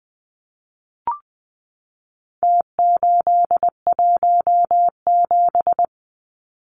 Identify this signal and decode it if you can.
Morse code, heard as a keyed tone, about 20 wpm, 700 Hz. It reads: T817
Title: Numbers - Contest Style - 4-Digit 20wpm